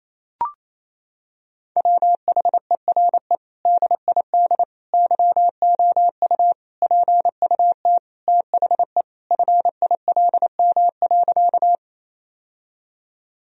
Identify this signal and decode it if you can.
Morse code, heard as a keyed tone, about 28 wpm, 700 Hz. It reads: WHERE DID YOU PUT THE FILM.
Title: Sentences from Top 1000 Words - Review Part 3 - 28wpm